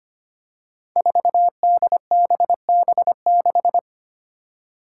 Morse code, heard as a keyed tone, about 25 wpm, 700 Hz. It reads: 4DBB6
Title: Five Challenging Letter-Number Combinations 25wpm